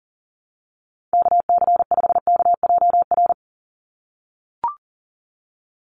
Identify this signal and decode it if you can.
Morse code, heard as a keyed tone, about 40 wpm, 700 Hz. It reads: KC5KJR
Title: USA Call Signs 40wpm